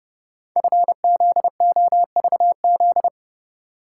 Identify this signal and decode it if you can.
Morse code, heard as a keyed tone, about 31 wpm, 690 Hz. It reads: FZOVZ
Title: Code Groups - 5 Letters - 30wpm